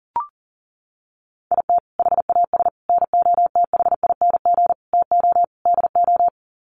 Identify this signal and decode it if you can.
Morse code, heard as a keyed tone, about 40 wpm, 700 Hz. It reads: IT HAS NOTHING TO DO
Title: Sets of 5 Words Encore 40wpm